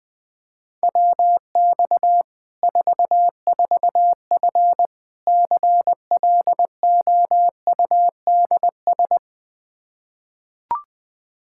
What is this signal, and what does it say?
Morse code, heard as a keyed tone, about 20 wpm, 700 Hz. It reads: WX 44F CLOUDS
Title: K1USN SST QSO Practice 20wpm